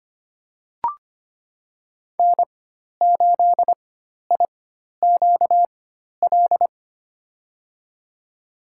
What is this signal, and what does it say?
Morse code, heard as a keyed tone, about 25 wpm, spaced as for 12 wpm, 700 Hz. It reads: N8IQL
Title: USA Call Signs - 5 Characters - 4x Character Spacing - 25wpm